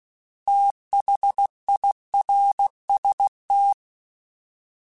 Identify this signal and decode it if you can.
Morse code, heard as a keyed tone, about 16 wpm, 800 Hz. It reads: THIRST
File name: TH / TH mp3s / Fast TH @15